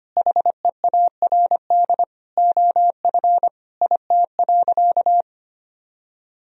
Morse code, heard as a keyed tone, about 25 wpm, 700 Hz. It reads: HEARD OF IT.
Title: Sentences from Top 1000 Words - Review Part 3 - 25wpm